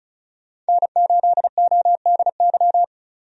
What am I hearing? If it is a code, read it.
Morse code, heard as a keyed tone, about 34 wpm, 700 Hz. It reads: N8ODY
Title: USA Call Signs - 5 Characters - 35wpm